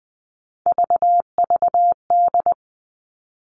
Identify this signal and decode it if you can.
Morse code, heard as a keyed tone, about 20 wpm, 700 Hz. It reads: VVD
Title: Three Challenging Letter-Number Combinations 20wpm